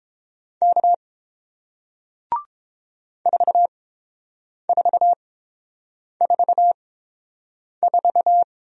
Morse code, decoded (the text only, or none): K 4444